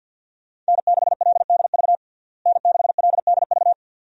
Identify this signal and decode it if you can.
Morse code, heard as a keyed tone, about 47 wpm, 700 Hz. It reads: N6LDV N6LDV
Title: USA Call Signs - 5 Characters - Repeated 2x - 50wpm